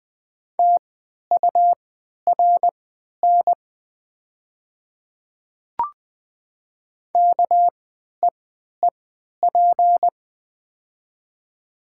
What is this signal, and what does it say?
Morse code, heard as a keyed tone, about 20 wpm, spaced as for 11 wpm, 700 Hz. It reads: TURN KEEP
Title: 3x Character Spacing - Top 300 Words - 20wpm